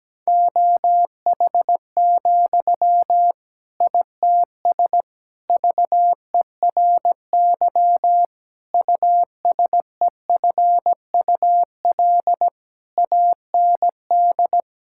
Morse code, heard as a keyed tone, about 17 wpm, 700 Hz. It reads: OH, ITS VERY USEFUL AND
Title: Sentences from Top 1000 Words - Review Part 6 - 17wpm